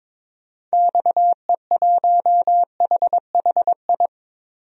Morse code, heard as a keyed tone, about 22 wpm, 700 Hz. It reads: XE1HHI
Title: DXCC-100 Call Signs - 6 Characters - Repeated 3x - 22wpm